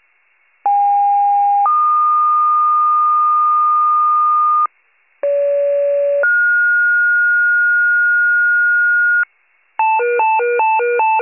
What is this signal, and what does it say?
Two-tone page, heard as a one-second tone, then a three-second tone, about 810 then 1220 Hz, about 570 then 1430 Hz, with noise. It is heard from 0.7 to 9.2 s.
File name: pager / 29 FIRE / wcalert